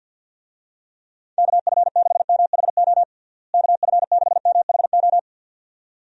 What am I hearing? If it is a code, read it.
Morse code, heard as a keyed tone, about 50 wpm, 700 Hz. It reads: KF6MHO KF6MHO
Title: USA Call Signs - 6 Characters - Repeated 2x - 50wpm